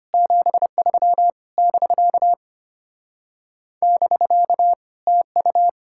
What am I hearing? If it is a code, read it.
Morse code, heard as a keyed tone, about 33 wpm, 700 Hz. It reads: 73 <BK> <BK> TU